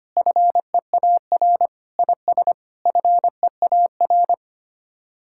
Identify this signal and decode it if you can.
Morse code, heard as a keyed tone, about 25 wpm, 700 Hz. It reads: FEAR IS FEAR
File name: Sets of 9 Words - 25wpm